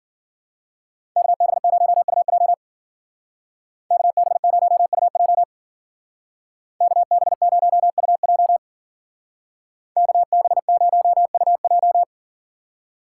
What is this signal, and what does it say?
Morse code, heard as a keyed tone, about 50 wpm, 700 Hz. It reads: KB0UJ KB0UJ KB0UJ KB0UJ
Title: Speed Racing - USA Call Signs 40wpm